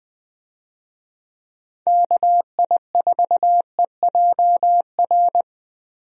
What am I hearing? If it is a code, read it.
Morse code, heard as a keyed tone, about 20 wpm, 700 Hz. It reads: KI4EJR
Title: USA Call Signs - 6 Characters - Repeated 3x - 20wpm